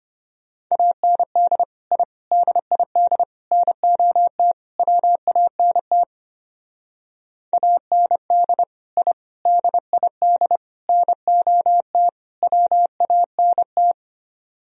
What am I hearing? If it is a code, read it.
Morse code, heard as a keyed tone, about 31 wpm, 700 Hz. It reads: AND I DID NOT WANT AND I DID NOT WANT
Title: Speed-Racing - Sets of 5 Words 25wpm